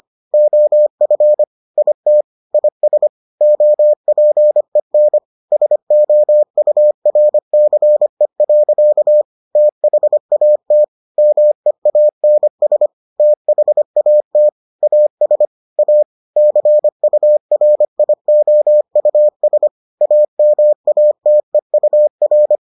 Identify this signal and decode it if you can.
Morse code, heard as a keyed tone, about 25 wpm, 600 Hz. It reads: OF IT IS OPEN SOURCE. THAT MEANS THAT AS A CURIOUS AMATEUR